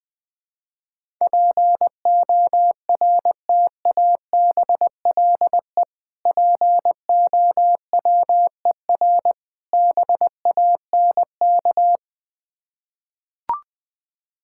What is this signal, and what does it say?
Morse code, heard as a keyed tone, about 21 wpm, 700 Hz. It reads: PORTABLE POWER BANK
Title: Difficult Words with Hints - 20wpm